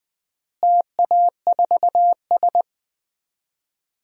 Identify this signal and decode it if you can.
Morse code, heard as a keyed tone, about 21 wpm, 700 Hz. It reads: TA4S
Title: DXCC-100 Call Signs - 4 Characters - Repeated 3x - 20wpm